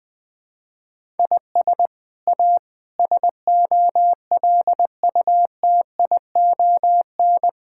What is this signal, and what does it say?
Morse code, heard as a keyed tone, about 20 wpm, 700 Hz. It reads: IS A SOLUTION
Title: Sets of 8 Words - 20wpm